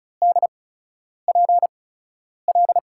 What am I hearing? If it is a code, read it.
Morse code, heard as a keyed tone, about 36 wpm, spaced as for 10 wpm, 700 Hz. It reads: DPL
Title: USA Call Signs - 8x Character Spacing - 35wpm